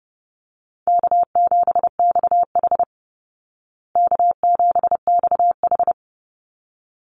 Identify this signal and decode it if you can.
Morse code, heard as a keyed tone, about 30 wpm, 700 Hz. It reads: K7XH K7XH
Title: Call Signs Repeated 3x - 30wpm